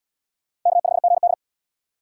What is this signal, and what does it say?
Morse code, heard as a keyed tone, about 68 wpm, 700 Hz. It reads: K5GD